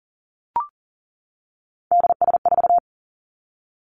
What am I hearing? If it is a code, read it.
Morse code, heard as a keyed tone, about 40 wpm, 700 Hz. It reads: DS4
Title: Three Challenging Letter-Number Combinations 40wpm